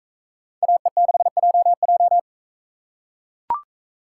Morse code, heard as a keyed tone, about 41 wpm, 700 Hz. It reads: AE6JJ